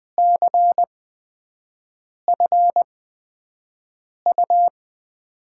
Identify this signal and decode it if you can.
Morse code, heard as a keyed tone, about 20 wpm, spaced as for 5 wpm, 700 Hz. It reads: CFU
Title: USA Call Signs - 6 Characters - 8x Character Spacing - 20wpm